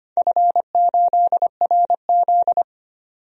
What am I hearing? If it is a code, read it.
Morse code, heard as a keyed tone, about 25 wpm, 700 Hz. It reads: F8RZ
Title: DXCC-100 Call Signs - 4 Characters - 25wpm